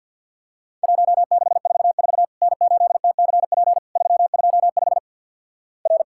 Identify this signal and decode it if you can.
Morse code, heard as a keyed tone, about 51 wpm, 700 Hz. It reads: 1644 N8TCP 325 R
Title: Worked All Europe DX Contest - QTC Exchange with Repeats - 50wpm